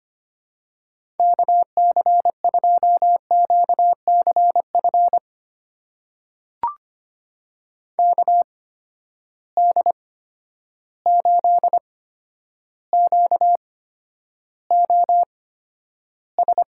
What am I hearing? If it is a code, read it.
Morse code, heard as a keyed tone, about 25 wpm, spaced as for 11 wpm, 700 Hz. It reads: KC2QCF KD8QOS
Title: USA Call Signs - 6 Characters - 8x Character Spacing - 25wpm